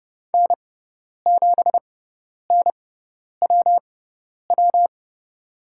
Morse code, heard as a keyed tone, about 30 wpm, spaced as for 10 wpm, 700 Hz. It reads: N7NWW
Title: USA Call Signs - 5 Characters - 6x Character Spacing - 30wpm